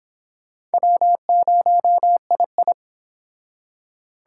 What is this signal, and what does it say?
Morse code, heard as a keyed tone, about 25 wpm, 700 Hz. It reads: W0II